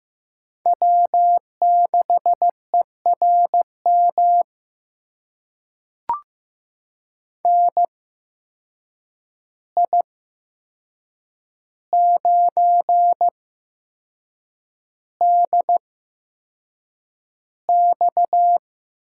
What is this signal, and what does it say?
Morse code, heard as a keyed tone, about 15 wpm, spaced as for 6 wpm, 700 Hz. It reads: W6ERM NI9DX
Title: USA Call Signs - 8x Character Spacing - 15wpm